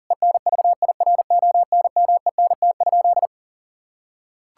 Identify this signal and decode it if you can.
Morse code, heard as a keyed tone, about 40 wpm, 700 Hz. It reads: ENVIRONMENT?
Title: Sentences from Top 1000 Words - Review Part 5 - 40wpm